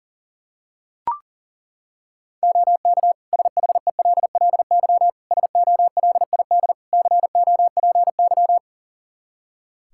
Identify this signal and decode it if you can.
Morse code, heard as a keyed tone, about 37 wpm, 700 Hz. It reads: OK SHELLY SOLID COPY
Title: Practice QSO - Exchange 3 - More Info - 40wpm